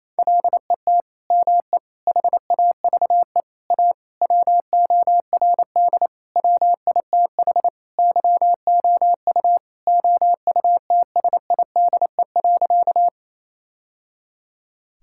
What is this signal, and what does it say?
Morse code, heard as a keyed tone, about 28 wpm, 700 Hz. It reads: LET ME HAVE A WORD WITH YOU OUTSIDE.
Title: Sentences from Top 1000 Words - Review Part 6 - 28wpm